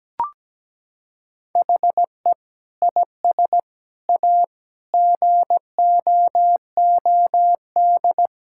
Morse code, heard as a keyed tone, about 17 wpm, 700 Hz. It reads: HE IS A GOOD
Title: Sentences from Top 1600 Words 17wpm